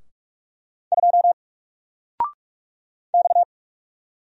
Morse code, heard as a keyed tone, about 46 wpm, 700 Hz. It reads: 2 X
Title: Single Character (Letter:Number:Punctuation:Prosign) 45wpm